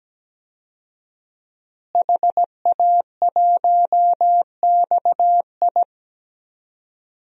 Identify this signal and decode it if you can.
Morse code, heard as a keyed tone, about 17 wpm, 700 Hz. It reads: HA1XI
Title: DXCC-100 Call Signs - 5 Characters - Repeated 3x - 17wpm